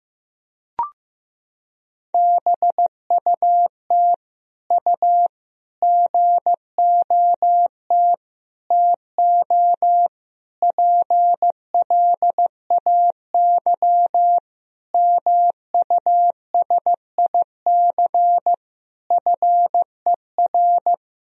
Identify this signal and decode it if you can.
Morse code, heard as a keyed tone, about 15 wpm, 700 Hz. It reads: BUT U GOT TO PLAY MUSIC FER
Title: Sentences from Top 300 Words and Abbreviations 15wpm